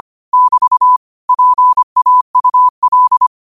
Morse code, heard as a keyed tone, about 25 wpm, 1000 Hz. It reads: <BT> PAUL
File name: qotdfiles / 25wpm